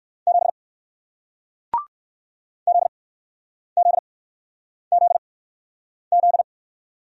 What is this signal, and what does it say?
Morse code, heard as a keyed tone, about 61 wpm, spaced as for 9 wpm, 700 Hz. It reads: 7 ZZZZ